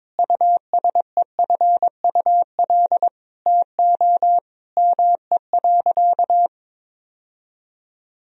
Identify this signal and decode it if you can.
Morse code, heard as a keyed tone, about 22 wpm, 700 Hz. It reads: USEFUL TO ME.
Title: Sentences from Top 1000 Words - Review Part 4 - 22wpm